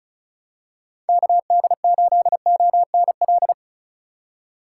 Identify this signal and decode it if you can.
Morse code, heard as a keyed tone, about 35 wpm, 700 Hz. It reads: KD8ONL
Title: USA Call Signs - 6 Characters - 6x Character Spacing - 35wpm